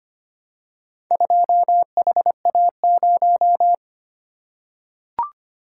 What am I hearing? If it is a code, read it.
Morse code, heard as a keyed tone, about 25 wpm, 700 Hz. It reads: 2HA0
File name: practice / 25wpm / Four Letter-Number Combinations - 25wpm